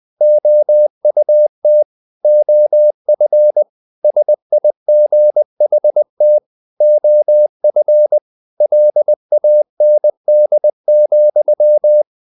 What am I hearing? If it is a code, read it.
Morse code, heard as a keyed tone, about 20 wpm, 600 Hz. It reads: OUT OF SIGHT OF LAND,
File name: the old man and the sea / the old man and the sea part 1 20wpm